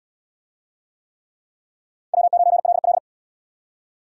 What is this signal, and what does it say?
Morse code, heard as a keyed tone, about 62 wpm, 700 Hz. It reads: W9LL
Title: Speed Racing - USA 1x2 Call Signs 50wpm